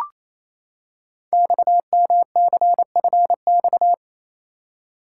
Morse code, heard as a keyed tone, about 29 wpm, 700 Hz. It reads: XMCFX
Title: Code Groups - 5 Letters - 28wpm